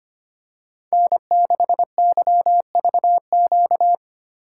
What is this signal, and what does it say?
Morse code, heard as a keyed tone, about 25 wpm, 700 Hz. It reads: N6YVQ
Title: USA Call Signs - 5 Characters - 2x Character Spacing - 25wpm